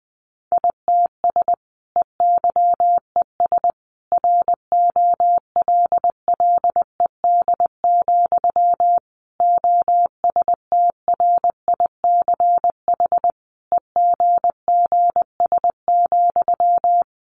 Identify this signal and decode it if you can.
Morse code, heard as a keyed tone, about 20 wpm, 700 Hz. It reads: ITS EYES ROLLED, OSTRICH EGGS,
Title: A Sound Of Thunder 20wpm